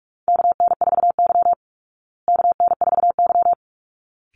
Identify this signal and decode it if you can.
Morse code, heard as a keyed tone, about 44 wpm, 700 Hz. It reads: KN4Y KN4Y
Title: Call Signs Repeated 3x - 45wpm